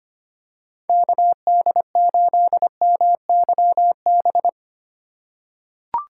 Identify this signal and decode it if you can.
Morse code, heard as a keyed tone, about 25 wpm, 700 Hz. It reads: KD8MYB
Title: USA Call Signs - 6 Characters - 2x Character Spacing - 25wpm